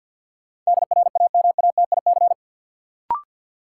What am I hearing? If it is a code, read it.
Morse code, heard as a keyed tone, about 52 wpm, 700 Hz. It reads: DRAMATIC
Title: Sentences from Top 1300 Words - New Word Drill - 50wpm